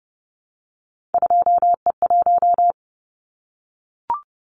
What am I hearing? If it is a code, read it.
Morse code, heard as a keyed tone, about 30 wpm, 700 Hz. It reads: 2E1
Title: Callsign Prefixes for Top 25 DXCC - Repeated 3x - 30wpm